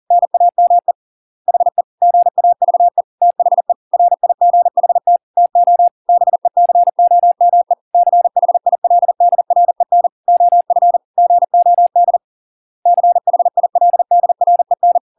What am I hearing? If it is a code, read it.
Morse code, heard as a keyed tone, about 40 wpm, 700 Hz. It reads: NAME HE GAVE THE RIGHT TO BECOME CHILDREN OF GOD CHILDREN